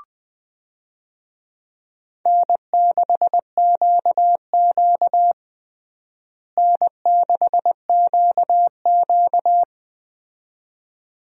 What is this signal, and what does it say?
Morse code, heard as a keyed tone, about 21 wpm, 700 Hz. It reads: N6QQ N6QQ
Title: USA Call Signs - 4 Characters - Repeated 2x - 20wpm